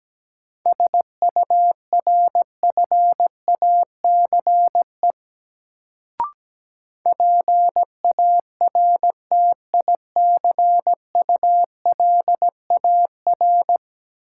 Morse code, heard as a keyed tone, about 17 wpm, 700 Hz. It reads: SURFACE PARTICULAR
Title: Sentences from Top 600 Words - New Word Drill - 17wpm